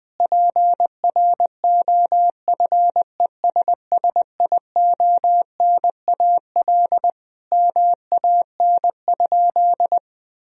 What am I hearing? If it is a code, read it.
Morse code, heard as a keyed tone, about 20 wpm, 700 Hz. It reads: PROFESSIONAL MAN?
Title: Sentences from Top 1000 Words - Review Part 4 - 20wpm